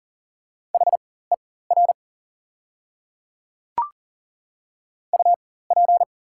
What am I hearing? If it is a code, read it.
Morse code, heard as a keyed tone, about 40 wpm, spaced as for 19 wpm, 690 Hz. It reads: HER UP
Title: 4x Character Spacing - Top 100 Words - 40wpm